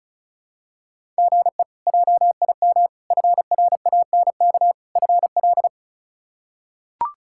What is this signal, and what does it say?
Morse code, heard as a keyed tone, about 34 wpm, 700 Hz. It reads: GE JIM FRANK FL
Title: K1USN SST QSO Practice - 35wpm